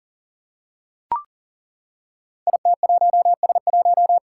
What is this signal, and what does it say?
Morse code, heard as a keyed tone, about 38 wpm, 700 Hz. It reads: IT1S1
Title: Five Challenging Letter-Number Combinations 40wpm